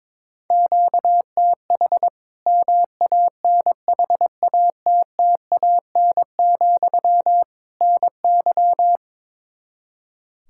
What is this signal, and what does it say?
Morse code, heard as a keyed tone, about 22 wpm, 700 Hz. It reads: QTH MANHATTAN, NY
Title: Practice QSO - Exchange 1 - Beginning - 22wpm